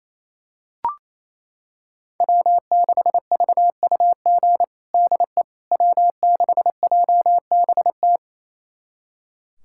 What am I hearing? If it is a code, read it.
Morse code, heard as a keyed tone, about 28 wpm, 700 Hz. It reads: W6VUG DE W6JBT
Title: USA Callsign DE Callsign - 5 Characters - 28wpm